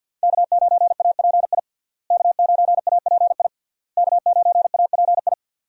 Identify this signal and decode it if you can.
Morse code, heard as a keyed tone, about 52 wpm, 700 Hz. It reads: K9API K9API K9API
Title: USA Call Signs - 5 Characters - Repeated 3x - 50wpm